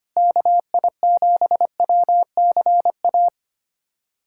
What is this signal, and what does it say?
Morse code, heard as a keyed tone, about 25 wpm, 700 Hz. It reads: KI7WCA
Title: USA Callsign DE Callsign 25wpm